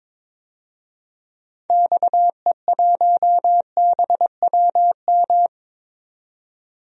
Morse code, heard as a keyed tone, about 21 wpm, 700 Hz. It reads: XE1BWM
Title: DXCC-100 Call Signs - 6 Characters - 22wpm